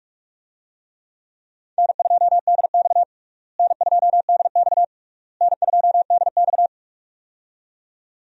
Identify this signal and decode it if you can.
Morse code, heard as a keyed tone, about 46 wpm, 700 Hz. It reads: N2DX N2DX N2DX
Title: USA Call Signs - 4 Characters - Repeated 3x - 45wpm